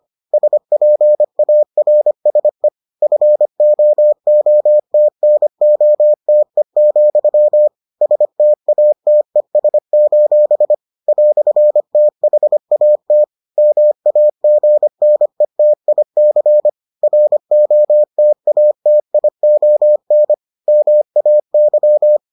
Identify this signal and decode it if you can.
Morse code, heard as a keyed tone, about 25 wpm, 600 Hz. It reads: SPARSE FOOTNOTE, STATES: "THAT MAGNETIC ROTATION MAY